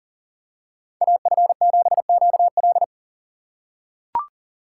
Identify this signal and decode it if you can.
Morse code, heard as a keyed tone, about 40 wpm, 700 Hz. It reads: AF7QL
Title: USA Call Signs - 5 Characters - 4x Character Spacing - 40wpm